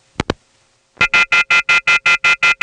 Off-hook warning tone, 1400 + 2060 + 2450 + 2600 Hz.